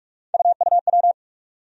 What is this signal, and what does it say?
Morse code, heard as a keyed tone, about 47 wpm, 700 Hz. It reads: UUW
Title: USA Call Sign Postfixes - 3 Letters 45wpm